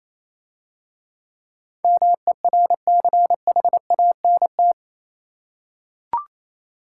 Morse code, heard as a keyed tone, about 27 wpm, 700 Hz. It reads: MERCHANT
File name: Easy Words - Speed-Racing with Hints - 22wpm